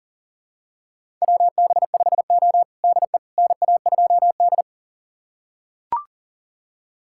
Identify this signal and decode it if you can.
Morse code, heard as a keyed tone, about 37 wpm, 690 Hz. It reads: WB5O DE NA2D
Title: USA Callsign DE Callsign - 4 Characters - 40wpm